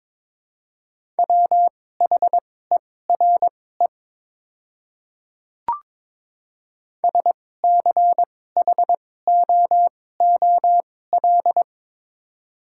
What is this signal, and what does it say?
Morse code, heard as a keyed tone, about 22 wpm, spaced as for 16 wpm, 700 Hz. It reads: WHERE SCHOOL
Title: 2x Character Spacing - Top 500 Words - 22wpm